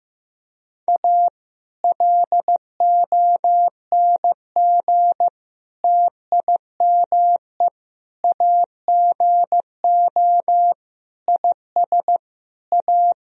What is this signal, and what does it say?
Morse code, heard as a keyed tone, about 15 wpm, 700 Hz. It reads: A LONG TIME AGO IS A
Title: Sentences from Top 1000 Words - Review Part 3 - 15wpm